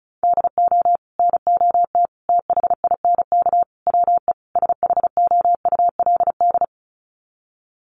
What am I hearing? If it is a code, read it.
Morse code, heard as a keyed tone, about 34 wpm, 700 Hz. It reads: DO NOT THINK WE SHOULD
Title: Sets of 5 Words Encore 35wpm